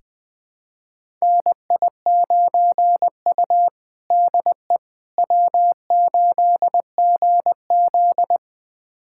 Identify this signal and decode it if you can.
Morse code, heard as a keyed tone, about 20 wpm, 700 Hz. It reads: NI9U DE W8GZ
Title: USA Callsign DE Callsign - 4 Characters - 20wpm